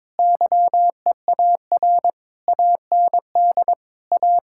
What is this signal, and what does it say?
Morse code, heard as a keyed tone, about 22 wpm, 700 Hz. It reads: YEAR AND A
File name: practice / 22wpm / Sentences with 4-Word Phrases - 22wpm